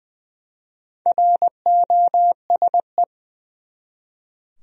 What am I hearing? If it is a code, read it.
Morse code, heard as a keyed tone, about 20 wpm, 700 Hz. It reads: ROSE
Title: USA Common Names - Max 4 Letters - 20wpm